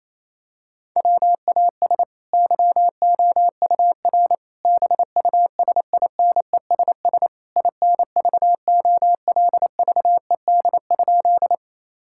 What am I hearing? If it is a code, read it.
Morse code, heard as a keyed tone, about 28 wpm, 700 Hz. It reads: WAS YOUR BUSINESS INVOLVED?
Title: Sentences from Top 1000 Words - Review Part 5 - 28wpm